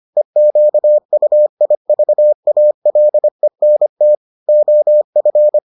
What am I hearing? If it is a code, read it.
Morse code, heard as a keyed tone, about 25 wpm, 600 Hz. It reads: EQUIVALENT OF